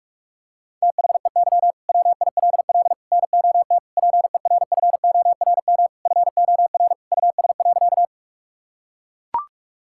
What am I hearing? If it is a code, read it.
Morse code, heard as a keyed tone, about 41 wpm, 700 Hz. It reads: THEY WILL NOT PERFORM FOR US.
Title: Sentences from Top 1000 Words - Review Part 5 - 45wpm